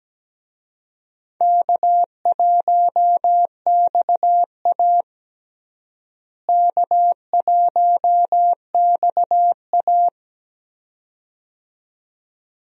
Morse code, heard as a keyed tone, about 17 wpm, 700 Hz. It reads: K1XA K1XA
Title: USA Call Signs - 4 Characters - Repeated 2x - 17wpm